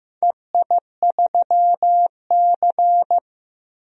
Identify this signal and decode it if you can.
Morse code, heard as a keyed tone, about 15 wpm, 700 Hz. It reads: EI3C
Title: DXCC-100 Call Signs - 4 Characters - 15wpm